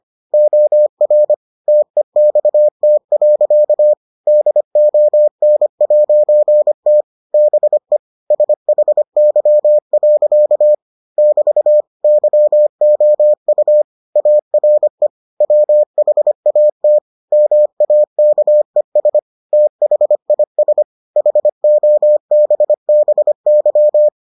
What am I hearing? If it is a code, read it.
Morse code, heard as a keyed tone, about 25 wpm, 600 Hz. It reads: OR TEXT. DON'T BE SHY. <BT> YOU ARE WHAT MAKES THIS HOBBY